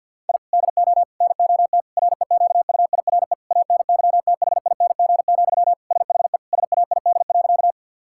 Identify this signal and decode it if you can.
Morse code, heard as a keyed tone, about 46 wpm, 700 Hz. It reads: I DO NOT REQUIRE ANYTHING, SHE SAID.
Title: Sentences from Top 1000 Words - Review Part 6 - 50wpm